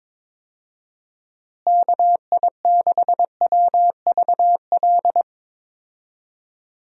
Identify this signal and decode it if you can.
Morse code, heard as a keyed tone, about 22 wpm, 700 Hz. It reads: KI6WVL
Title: USA Call Signs - 6 Characters - Repeated 3x - 22wpm